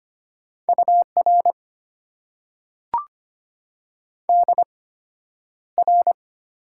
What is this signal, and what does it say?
Morse code, heard as a keyed tone, about 25 wpm, spaced as for 11 wpm, 700 Hz. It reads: UR DR
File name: Top 90 QSO Elements - 8x Character Spacing - 25wpm